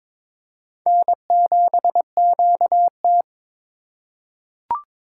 Morse code, heard as a keyed tone, about 22 wpm, 700 Hz. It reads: N7QT
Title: K1USN SST QSO Practice - 22wpm